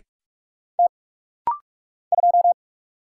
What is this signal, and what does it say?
Morse code, heard as a keyed tone, about 44 wpm, 700 Hz. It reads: T 2
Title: Single Letter-Number - Rapid-Fire 45wpm